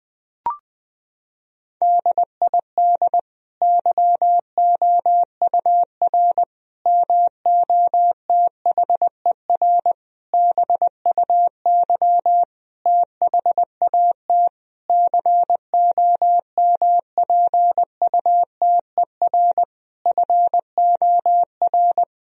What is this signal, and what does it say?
Morse code, heard as a keyed tone, about 20 wpm, 700 Hz. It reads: DID YOUR MOTHER BUY THAT COMPUTER FOR
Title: Sentences from Top 1000 Words - Review Part 3 - 20wpm